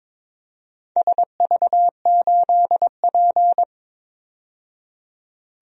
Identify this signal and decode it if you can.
Morse code, heard as a keyed tone, about 22 wpm, 700 Hz. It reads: SV8P
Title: DXCC-100 Call Signs - 4 Characters - 22wpm